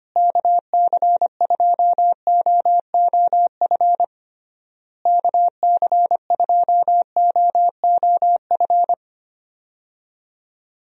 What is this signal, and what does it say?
Morse code, heard as a keyed tone, about 25 wpm, 700 Hz. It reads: KC2OOF KC2OOF
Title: USA Call Signs - 6 Characters - Repeated 3x - 25wpm